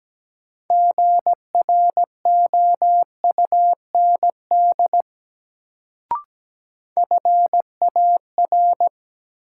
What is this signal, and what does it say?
Morse code, heard as a keyed tone, about 17 wpm, 700 Hz. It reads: GROUND FAR
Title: Sets of 3 Words - New Word Drill - 17wpm